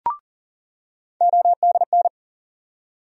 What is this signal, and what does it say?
Morse code, heard as a keyed tone, about 40 wpm, 700 Hz. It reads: ODN
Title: USA Call Sign Postfixes - 3 Letters 40wpm